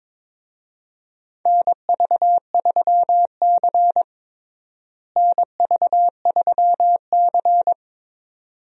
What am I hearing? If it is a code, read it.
Morse code, heard as a keyed tone, about 23 wpm, 700 Hz. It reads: NV3C NV3C
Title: USA Call Signs - 4 Characters - Repeated 2x - 22wpm